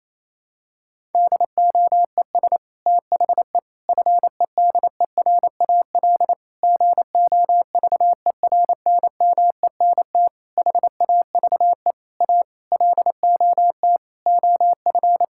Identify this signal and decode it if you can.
Morse code, heard as a keyed tone, about 28 wpm, 700 Hz. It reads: DOES THE FEDERAL GOVERNMENT HAVE A LOT OF
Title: Sentences from Top 1000 Words - Review Part 5 - 28wpm